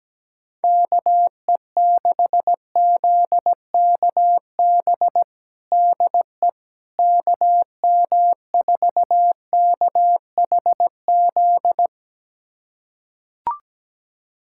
Morse code, heard as a keyed tone, about 17 wpm, 700 Hz. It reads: KE6ZKB DE KM4KHZ
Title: USA Callsign DE Callsign 17wpm